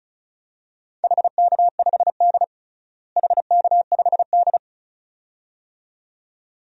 Morse code, heard as a keyed tone, about 36 wpm, 700 Hz. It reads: HK5D HK5D
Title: DXCC-100 Call Signs - 4 Characters - Repeated 2x - 35wpm